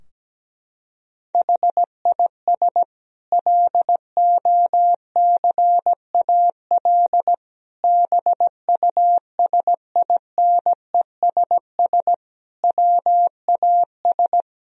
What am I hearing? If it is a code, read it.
Morse code, heard as a keyed tone, about 17 wpm, 700 Hz. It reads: HIS LOCAL BUSINESS WAS
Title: Sentences from Top 1000 Words - Review Part 3 - 17wpm